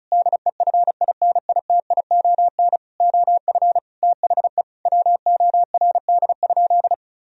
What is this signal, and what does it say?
Morse code, heard as a keyed tone, about 36 wpm, 700 Hz. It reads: DEFINITION OF THE WORD?
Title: Sentences from Top 1000 Words - Review Part 4 - 35wpm